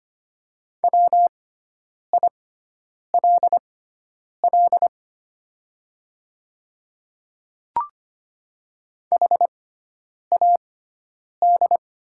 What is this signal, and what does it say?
Morse code, heard as a keyed tone, about 25 wpm, spaced as for 9 wpm, 700 Hz. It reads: WILL HAD
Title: 6x Character Spacing - Top 100 Words - 25wpm